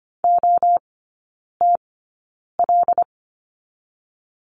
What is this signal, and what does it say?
Morse code, heard as a keyed tone, about 25 wpm, spaced as for 9 wpm, 700 Hz. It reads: OTL
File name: course / 25:10wpm / 093 Lesson 16e - Letter c - Three Characters 25wpm